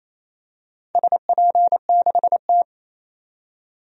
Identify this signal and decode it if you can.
Morse code, heard as a keyed tone, about 27 wpm, 700 Hz. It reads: SP6T
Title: DXCC-100 Call Signs - 4 Characters - 28wpm